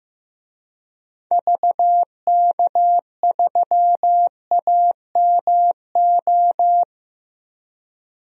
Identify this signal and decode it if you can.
Morse code, heard as a keyed tone, about 15 wpm, 700 Hz. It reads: VK3AMO
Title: DXCC-100 Call Signs - 6 Characters - Repeated 3x - 15wpm